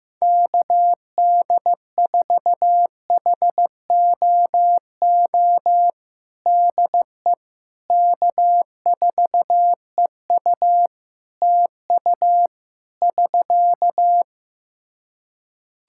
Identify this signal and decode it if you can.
Morse code, heard as a keyed tone, about 15 wpm, 700 Hz. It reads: KD4HOO DE K4EU TU <SK>
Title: Practice QSO - Exchange 4 - Ending - 15wpm